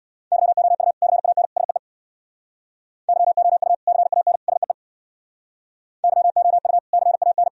Morse code, heard as a keyed tone, about 60 wpm, 700 Hz. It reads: YOU CAN SEE YOU CAN SEE YOU CAN
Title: Speed-Racing - Sets of 3 Words 50wpm